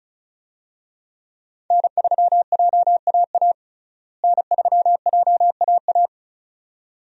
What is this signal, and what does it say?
Morse code, heard as a keyed tone, about 36 wpm, 700 Hz. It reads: N3JAA N3JAA
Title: USA Call Signs - 5 Characters - Repeated 2x - 35wpm